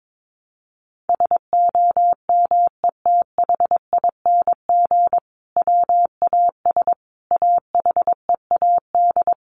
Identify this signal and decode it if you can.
Morse code, heard as a keyed tone, about 22 wpm, 700 Hz. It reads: SOMETHING WAS AHEAD
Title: Sentences from Top 1100 Words 22wpm